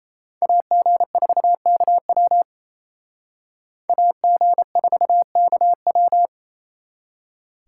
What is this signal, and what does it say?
Morse code, heard as a keyed tone, about 36 wpm, 700 Hz. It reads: AG4KW AG4KW